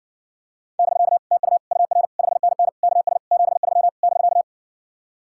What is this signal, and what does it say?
Morse code, heard as a keyed tone, about 69 wpm, 700 Hz. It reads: <BK> TU UR 5NN KS 73 <BK>